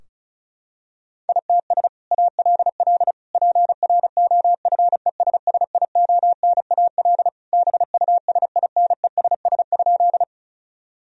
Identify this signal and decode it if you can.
Morse code, heard as a keyed tone, about 34 wpm, 700 Hz. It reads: ITS ALL PROFESSIONAL BUSINESS?
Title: Sentences from Top 1000 Words - Review Part 4 - 35wpm